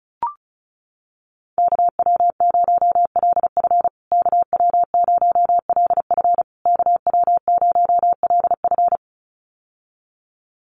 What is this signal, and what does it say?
Morse code, heard as a keyed tone, about 34 wpm, 700 Hz. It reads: KW0LF KW0LF KW0LF
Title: USA Call Signs Repeated 3x 35wpm